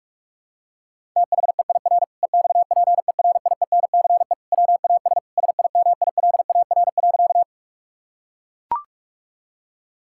Morse code, heard as a keyed tone, about 41 wpm, 700 Hz. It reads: THEIR EXPERIENCE WAS SIMILAR.
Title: Sentences from Top 1000 Words - Review Part 5 - 45wpm